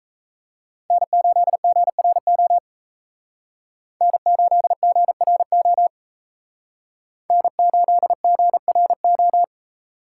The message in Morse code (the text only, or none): N8GRO N8GRO N8GRO